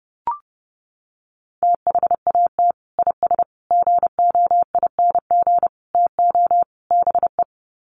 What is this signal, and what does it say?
Morse code, heard as a keyed tone, about 30 wpm, 700 Hz. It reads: THAT IS GOING TO BE
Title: Sets of 5 Words Encore2 30wpm